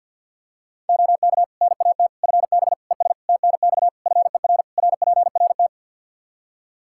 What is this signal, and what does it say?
Morse code, heard as a keyed tone, about 52 wpm, 700 Hz. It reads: OK NAT FB ES TNX FER RPRT
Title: Practice QSO - Exchange 2 - Info - 50wpm